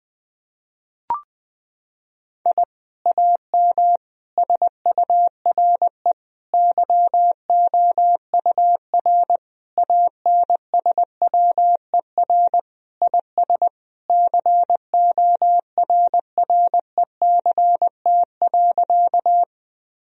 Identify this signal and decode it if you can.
Morse code, heard as a keyed tone, about 20 wpm, 700 Hz. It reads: I AM SURE YOUR ANSWER IS CORRECT.
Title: Sentences from Top 1000 Words - Review Part 2 - 20wpm